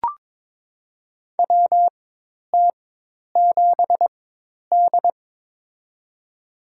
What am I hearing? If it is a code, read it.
Morse code, heard as a keyed tone, about 21 wpm, spaced as for 10 wpm, 700 Hz. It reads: WT7D
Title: USA Call Signs - 4x Character Spacing - 22wpm